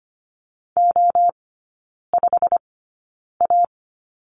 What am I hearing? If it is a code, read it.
Morse code, heard as a keyed tone, about 25 wpm, spaced as for 9 wpm, 700 Hz. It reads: O5A
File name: course / 25:10wpm / 093 Lesson 16e - Letter c - Three Characters 25wpm